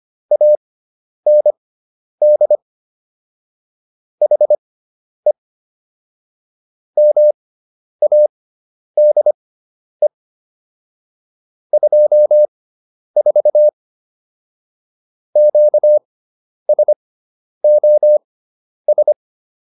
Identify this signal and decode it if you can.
Morse code, heard as a keyed tone, about 25 wpm, spaced as for 10 wpm, 600 Hz. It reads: AND HE MADE 24 QSOS